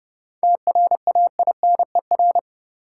Morse code, heard as a keyed tone, about 30 wpm, 700 Hz. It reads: TRAINER
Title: Seven Letter Words - 30wpm